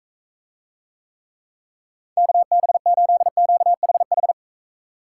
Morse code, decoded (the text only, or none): KB8QHH